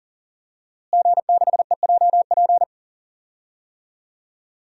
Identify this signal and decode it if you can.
Morse code, heard as a keyed tone, about 39 wpm, 700 Hz. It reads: G6EJP
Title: DXCC-100 Call Signs - 5 Characters - Repeated 2x - 40wpm